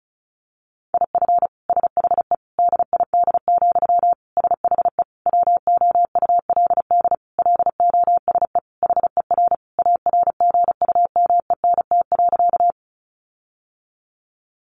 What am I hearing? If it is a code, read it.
Morse code, heard as a keyed tone, about 34 wpm, 700 Hz. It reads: IF SHE DID, SHE WOULD LOSE HER ARGUMENT.
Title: Sentences from Top 1400 Words 35wpm